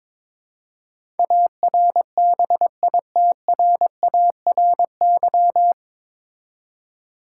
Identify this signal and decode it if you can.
Morse code, heard as a keyed tone, about 22 wpm, 700 Hz. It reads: ARBITRARY
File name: 9 Letter Words - First 3 Letters Given - 22wpm